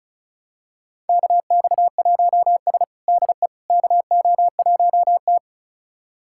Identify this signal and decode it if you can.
Morse code, heard as a keyed tone, about 36 wpm, 700 Hz. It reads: KX1S DE KO1T
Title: USA Callsign DE Callsign - 4 Characters - 35wpm